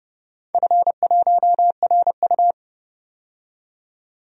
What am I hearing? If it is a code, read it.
Morse code, heard as a keyed tone, about 30 wpm, 700 Hz. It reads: F1RU
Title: DXCC-100 Call Signs - 4 Characters - 30wpm